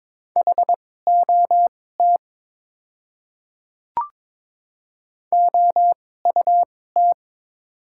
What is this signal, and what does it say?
Morse code, heard as a keyed tone, about 22 wpm, spaced as for 16 wpm, 690 Hz. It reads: HOT OUT
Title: 2x Character Spacing - Top 100 Words - 22wpm